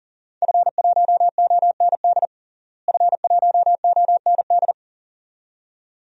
Morse code, heard as a keyed tone, about 40 wpm, 700 Hz. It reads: F1OND F1OND
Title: DXCC-100 Call Signs - 5 Characters - Repeated 2x - 40wpm